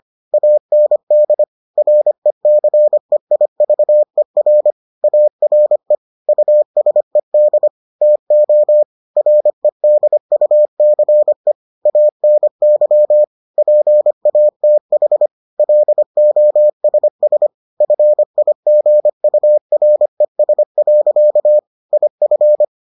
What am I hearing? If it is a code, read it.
Morse code, heard as a keyed tone, about 25 wpm, 600 Hz. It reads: AND RECEIVER ARE USED TO REDUCE ANY PATH LOSS FIGURES. IF